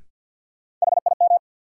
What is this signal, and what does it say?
Morse code, heard as a keyed tone, about 48 wpm, 700 Hz. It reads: HIM
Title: Sentences from Top 100 Words - New Word Drill - 50wpm